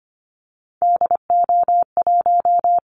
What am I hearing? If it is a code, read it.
Morse code, heard as a keyed tone, about 25 wpm, 700 Hz. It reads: DO1
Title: Callsign Prefixes for Top 25 DXCC - 25wpm